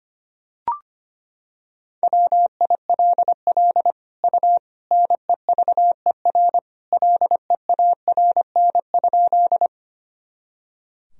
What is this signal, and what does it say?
Morse code, heard as a keyed tone, about 25 wpm, 700 Hz. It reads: WILL U NEVER LEARN?
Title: Sentences from Top 300 Words and Abbreviations 25wpm